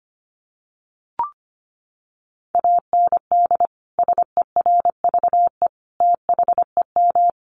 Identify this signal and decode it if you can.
Morse code, heard as a keyed tone, about 25 wpm, 700 Hz. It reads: AND SERVE THEM
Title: Sentences from Top 400 Words 25wpm